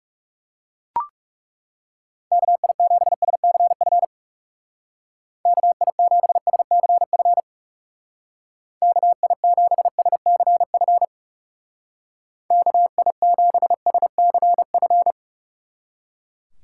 Morse code, heard as a keyed tone, about 35 wpm, 700 Hz. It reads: KI7SCF KI7SCF KI7SCF KI7SCF
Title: Speed Racing - USA Call Signs 30wpm